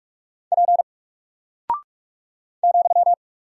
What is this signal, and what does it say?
Morse code, heard as a keyed tone, about 44 wpm, 700 Hz. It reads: P ,
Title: Single Character (Letter:Number:Punctuation:Prosign) 45wpm